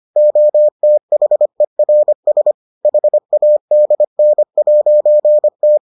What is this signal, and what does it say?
Morse code, heard as a keyed tone, about 25 wpm, 600 Hz. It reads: OTHERS HADN'T